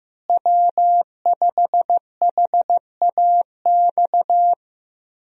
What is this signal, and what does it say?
Morse code, heard as a keyed tone, about 15 wpm, 700 Hz. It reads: W5HAX
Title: USA Call Signs - 5 Characters - Repeated 3x - 15wpm